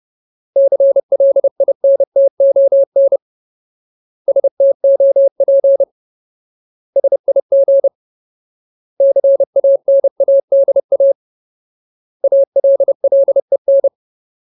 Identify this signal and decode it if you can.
Morse code, heard as a keyed tone, about 30 wpm, 550 Hz. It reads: CLINTON STOP SIG CANADA ALLEN
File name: Ultimate Word List / 30WPM Ultimate